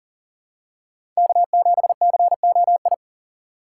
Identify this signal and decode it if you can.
Morse code, heard as a keyed tone, about 40 wpm, 700 Hz. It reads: K7COI
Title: USA Call Signs - 3x Character Spacing - 40wpm